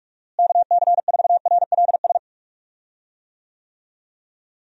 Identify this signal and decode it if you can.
Morse code, heard as a keyed tone, about 41 wpm, 700 Hz. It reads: KC4RLS
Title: USA Call Signs - 6 Characters - Repeated 2x - 45wpm